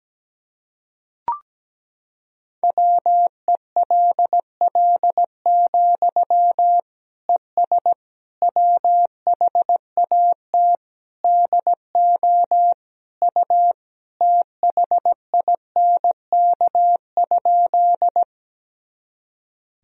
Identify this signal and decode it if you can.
Morse code, heard as a keyed tone, about 17 wpm, 700 Hz. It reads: WELL, ES WHAT DO U THINK?
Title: Sentences from Top 200 Words and Abbreviations 17wpm